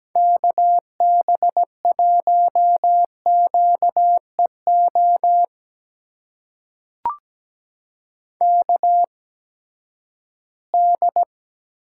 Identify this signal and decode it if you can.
Morse code, heard as a keyed tone, about 17 wpm, spaced as for 12 wpm, 700 Hz. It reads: KB1QEO KD
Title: USA Call Signs - 6 Characters - 8x Character Spacing - 17wpm